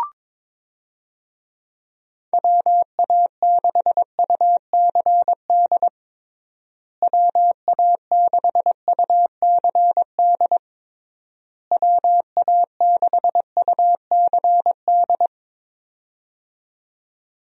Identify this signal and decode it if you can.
Morse code, heard as a keyed tone, about 22 wpm, 700 Hz. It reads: WA6UCD WA6UCD WA6UCD
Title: USA Call Signs - 6 Characters - Repeated 3x - 22wpm